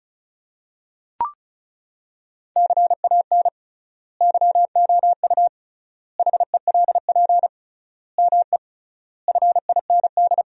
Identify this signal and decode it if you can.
Morse code, heard as a keyed tone, about 34 wpm, 700 Hz. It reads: CAN YOU HELP ME FIND
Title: 3x Word-Spacing - Sentences from Top 200 Words - 35wpm